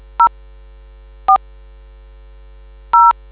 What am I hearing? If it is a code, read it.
DTMF, heard as telephone touch tones, tones of about 100 ms. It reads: 0 4 0